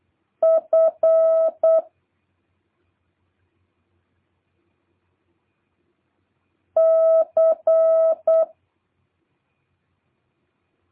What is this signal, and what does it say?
Morse code, heard as a keyed tone, about 8 wpm, 650 Hz. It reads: F C